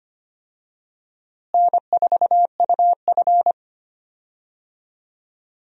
Morse code, heard as a keyed tone, about 25 wpm, 700 Hz. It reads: N4UF
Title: USA Call Signs - 4 Characters - Repeated 2x - 25wpm